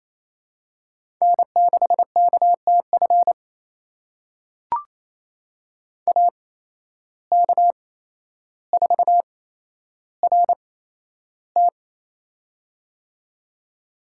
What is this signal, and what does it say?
Morse code, heard as a keyed tone, about 28 wpm, spaced as for 12 wpm, 700 Hz. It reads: N6KTF AK4RT
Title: USA Call Signs - 5 Characters - 8x Character Spacing - 28wpm